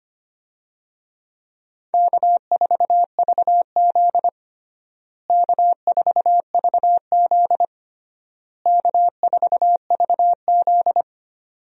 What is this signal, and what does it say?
Morse code, heard as a keyed tone, about 25 wpm, 700 Hz. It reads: K4VZ K4VZ K4VZ
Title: USA Call Signs - 4 Characters - Repeated 3x - 25wpm